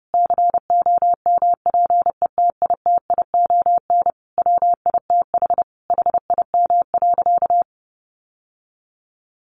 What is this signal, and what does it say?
Morse code, heard as a keyed tone, about 30 wpm, 700 Hz. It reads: COMPETITION WITH HIM.
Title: Sentences from Top 1000 Words 30wpm